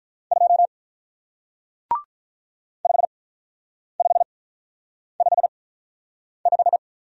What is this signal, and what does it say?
Morse code, heard as a keyed tone, about 46 wpm, spaced as for 9 wpm, 690 Hz. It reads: 2 5555